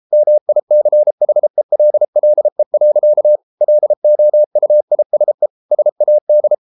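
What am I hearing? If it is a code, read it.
Morse code, heard as a keyed tone, about 33 wpm, 600 Hz. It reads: MICHELLE. LOUISE SAD
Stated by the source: Morse code at 33 wpm, QRQ.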